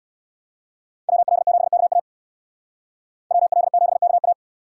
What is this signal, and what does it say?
Morse code, heard as a keyed tone, about 59 wpm, 700 Hz. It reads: WL7KA WL7KA
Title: Speed Racing - USA Call Signs 50wpm